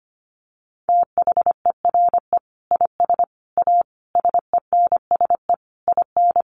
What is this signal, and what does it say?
Morse code, heard as a keyed tone, about 25 wpm, 700 Hz. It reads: THERE IS A SENSE IN
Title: Sentences from Top 1500 Words 25wpm